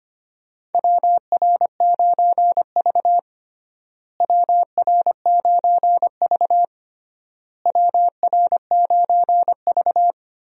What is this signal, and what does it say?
Morse code, heard as a keyed tone, about 25 wpm, 700 Hz. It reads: WR9V WR9V WR9V
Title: USA Call Signs - 4 Characters - Repeated 3x - 25wpm